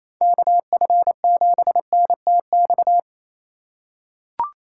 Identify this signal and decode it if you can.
Morse code, heard as a keyed tone, about 28 wpm, 700 Hz. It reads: KF7NTX
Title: USA Call Signs - 6 Characters - 3x Character Spacing - 28wpm